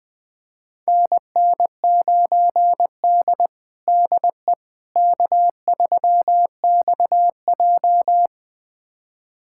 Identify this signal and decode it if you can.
Morse code, heard as a keyed tone, about 20 wpm, 700 Hz. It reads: NN9D DE K3XJ
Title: USA Callsign DE Callsign - 4 Characters - 20wpm